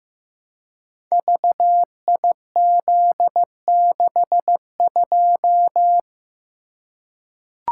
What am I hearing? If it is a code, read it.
Morse code, heard as a keyed tone, about 15 wpm, 700 Hz. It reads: VIZ62
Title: Code Groups - 5 Letter:Numbers - 15wpm